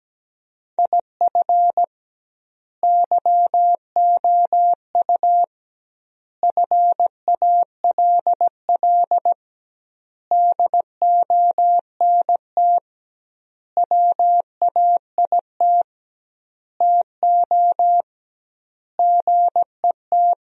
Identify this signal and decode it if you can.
Morse code, heard as a keyed tone, about 17 wpm, 700 Hz. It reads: IF YOU FALL DONT WAIT TO GET
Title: 2x Word-Spacing - Sentences from Top 500 Words - 17wpm